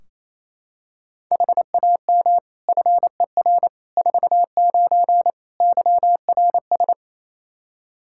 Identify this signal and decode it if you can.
Morse code, heard as a keyed tone, about 28 wpm, 700 Hz. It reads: HAM FER 49 YRS
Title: K1USN SST QSO Practice 28wpm